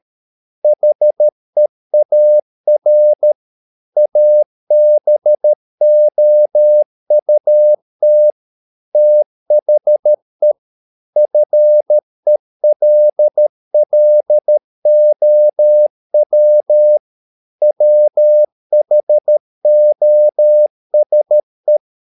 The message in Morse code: HEAR ABOUT THE FELLOW WHOSE